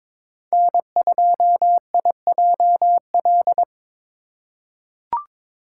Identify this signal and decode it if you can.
Morse code, heard as a keyed tone, about 22 wpm, 700 Hz. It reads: N2IJL
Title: USA Call Signs - 5 Characters - 4x Character Spacing - 22wpm